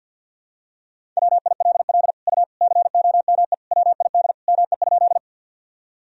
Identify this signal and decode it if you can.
Morse code, heard as a keyed tone, about 47 wpm, 700 Hz. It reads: WILL U COME WID ME?
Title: Sentences from Top 200 Words and Abbreviations 50wpm